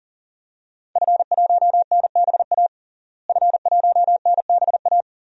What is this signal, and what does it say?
Morse code, heard as a keyed tone, about 40 wpm, 700 Hz. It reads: F1NBA F1NBA
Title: DXCC-100 Call Signs - 5 Characters - Repeated 2x - 40wpm